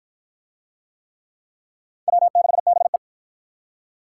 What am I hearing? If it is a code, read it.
Morse code, heard as a keyed tone, about 45 wpm, 700 Hz. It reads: W6BE